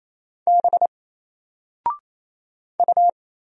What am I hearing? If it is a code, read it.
Morse code, heard as a keyed tone, about 28 wpm, 700 Hz. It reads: B U
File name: Single-Letter-Number-28wpm